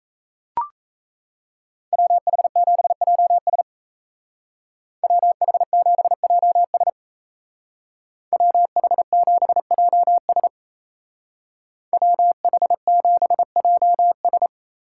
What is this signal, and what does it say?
Morse code, heard as a keyed tone, about 36 wpm, 700 Hz. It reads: WH7JS WH7JS WH7JS WH7JS